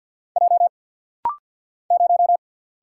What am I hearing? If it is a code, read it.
Morse code, heard as a keyed tone, about 48 wpm, 700 Hz. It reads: J 0
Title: Single Letter-Number - Rapid-Fire 50wpm